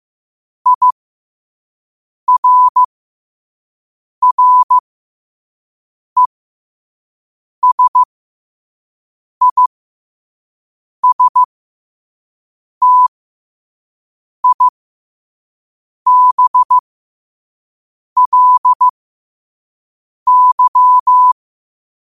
Morse code, heard as a keyed tone, about 15 wpm, spaced as for 5 wpm, 1000 Hz. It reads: IRRESISTIBLY
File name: qotdfiles / 5wpm